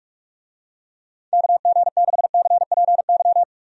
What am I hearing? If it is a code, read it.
Morse code, heard as a keyed tone, about 46 wpm, 700 Hz. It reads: KG6CPY
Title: USA Call Signs - 6 Characters - 45wpm